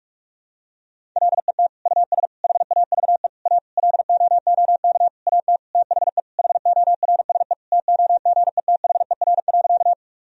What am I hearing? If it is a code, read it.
Morse code, heard as a keyed tone, about 48 wpm, 700 Hz. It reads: LET US HAVE A LOOK AT THE HORSE TOGETHER.
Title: Sentences from Top 1000 Words - Review Part 1 - 45wpm